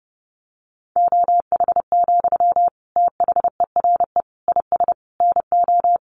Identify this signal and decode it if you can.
Morse code, heard as a keyed tone, about 30 wpm, 700 Hz. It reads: OH, THERE IS NO
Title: Sentences from Top 1500 Words 30wpm